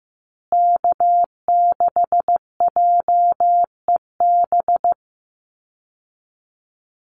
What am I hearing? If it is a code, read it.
Morse code, heard as a keyed tone, about 15 wpm, 700 Hz. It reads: K6JEB
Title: Call Signs Repeated 3x - 15wpm